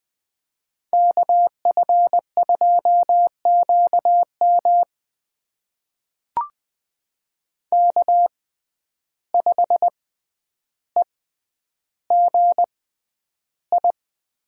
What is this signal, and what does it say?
Morse code, heard as a keyed tone, about 20 wpm, spaced as for 10 wpm, 700 Hz. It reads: KF2QM K5EGI
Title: USA Call Signs - 5 Characters - 6x Character Spacing - 20wpm